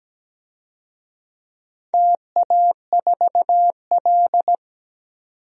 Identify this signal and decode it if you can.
Morse code, heard as a keyed tone, about 17 wpm, 700 Hz. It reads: TA4L
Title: DXCC-100 Call Signs - 4 Characters - 17wpm